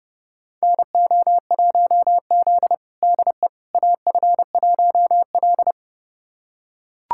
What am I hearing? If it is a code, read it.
Morse code, heard as a keyed tone, about 30 wpm, 700 Hz. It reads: NO1Z DE AF1L
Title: USA Callsign DE Callsign - 4 Characters - 30wpm